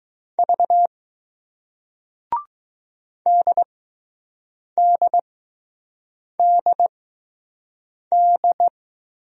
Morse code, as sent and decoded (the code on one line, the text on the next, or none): ...- / -.. -.. -.. -..
V DDDD